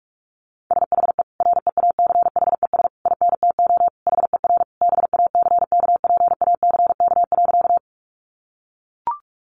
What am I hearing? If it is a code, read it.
Morse code, heard as a keyed tone, about 44 wpm, 700 Hz. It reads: SHE REACHES INTO HER BACKPACK.
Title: Additional Easy Phrases 45wpm